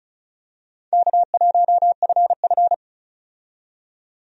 Morse code, heard as a keyed tone, about 36 wpm, 700 Hz. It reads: K1FF
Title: USA Call Signs - 4 Characters - 35wpm